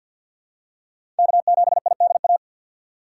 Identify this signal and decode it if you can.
Morse code, heard as a keyed tone, about 52 wpm, 690 Hz. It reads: K7IDA
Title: USA Call Signs - 5 Characters - 3x Character Spacing - 50wpm